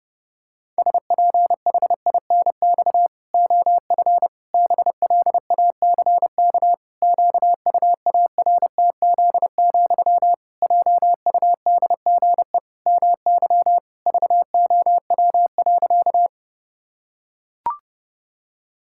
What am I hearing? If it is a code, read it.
Morse code, heard as a keyed tone, about 30 wpm, 700 Hz. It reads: SPHINX OF BLACK QUARTZ, JUDGE MY VOW.
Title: Pangram Challenge v2 - 30wpm